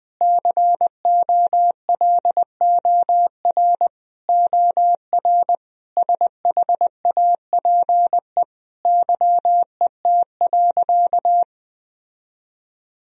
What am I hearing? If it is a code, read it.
Morse code, heard as a keyed tone, about 20 wpm, 700 Hz. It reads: COLOR OR SHAPE YET.
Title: Sentences from Top 1000 Words - Review Part 3 - 20wpm